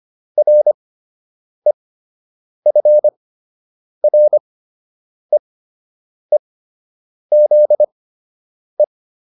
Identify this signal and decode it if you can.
Morse code, heard as a keyed tone, about 25 wpm, spaced as for 8 wpm, 600 Hz. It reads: REFREEZE